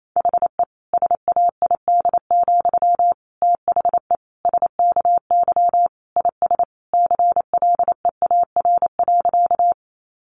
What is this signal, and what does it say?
Morse code, heard as a keyed tone, about 28 wpm, 700 Hz. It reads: HE SAID, THE SKY IS CLEAR.
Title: Sentences from Top 1200 Words 28wpm